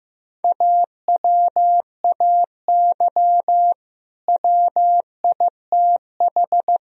Morse code, heard as a keyed tone, about 16 wpm, 700 Hz. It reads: AWAY WITH
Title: Sentences from Top 1000 Words - Review Part 4 - 15wpm